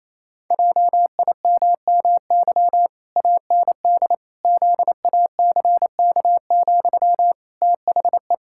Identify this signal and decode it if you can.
Morse code, heard as a keyed tone, about 28 wpm, 700 Hz. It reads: JIMMY AND ZACK, THE
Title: Pangram Challenge v2 - 28wpm